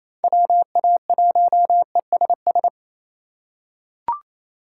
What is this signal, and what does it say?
Morse code, heard as a keyed tone, about 27 wpm, 700 Hz. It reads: WA1ESS
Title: USA Call Signs - 3x Character Spacing - 28wpm